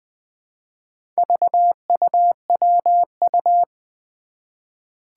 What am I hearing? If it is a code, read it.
Morse code, heard as a keyed tone, about 21 wpm, 700 Hz. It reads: VUWU
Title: Four Challenging Letter-Number Combinations 20wpm